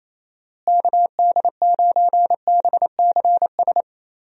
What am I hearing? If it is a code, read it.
Morse code, heard as a keyed tone, about 27 wpm, 700 Hz. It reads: KD9BCS
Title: USA Call Signs - 6 Characters - Repeated 3x - 28wpm